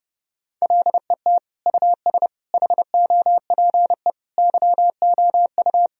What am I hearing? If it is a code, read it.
Morse code, heard as a keyed tone, about 30 wpm, 700 Hz. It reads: LET US HOPE YOU
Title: Sentences from Top 1000 Words - Review Part 1 - 30wpm